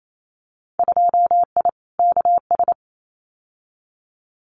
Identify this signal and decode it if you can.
Morse code, heard as a keyed tone, about 28 wpm, 700 Hz. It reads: 2I KS
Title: Winter Field Day Exchanges Encore 28wpm